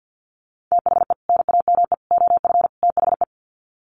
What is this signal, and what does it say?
Morse code, heard as a keyed tone, about 50 wpm, 700 Hz. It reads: THE NAME OF THE
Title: Sets of 4 Words 50wpm